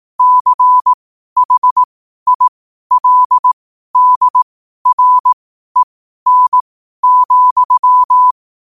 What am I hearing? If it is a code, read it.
Morse code, heard as a keyed tone, about 18 wpm, spaced as for 12 wpm, 1000 Hz. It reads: CHILDREN,
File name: qotdfiles / 12wpm